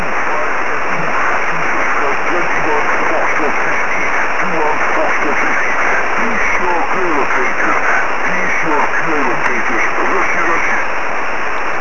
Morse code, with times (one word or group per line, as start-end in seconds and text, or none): none